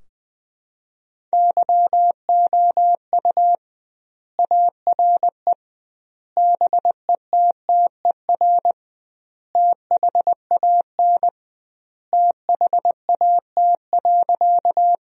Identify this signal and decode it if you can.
Morse code, heard as a keyed tone, about 21 wpm, 700 Hz. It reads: YOU ARE BETTER THAN THAT.
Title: 2x Word-Spacing - Sentences from Top 400 Words - 20wpm